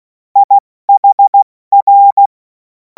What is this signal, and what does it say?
Morse code, heard as a keyed tone, about 16 wpm, 800 Hz. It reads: IHR
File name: Morsekurs / Witze / Witz 11. August 2026 70 F80